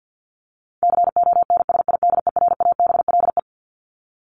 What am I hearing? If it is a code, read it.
Morse code, heard as a keyed tone, about 50 wpm, 700 Hz. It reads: CONSIDERABLE
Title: LY Suffix - Top 100 Words with Base Words 50wpm